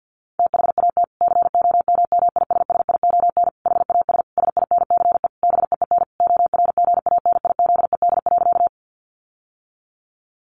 Morse code, heard as a keyed tone, about 50 wpm, 700 Hz. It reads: THAT COMMISSION HAS SINCE BEEN ORGANIZED.
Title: Sentences from Top 1400 Words 50wpm